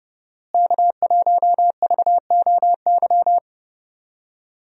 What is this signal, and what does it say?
Morse code, heard as a keyed tone, about 30 wpm, 700 Hz. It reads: K1VOY
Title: USA Call Signs - 5 Characters - Repeated 2x - 30wpm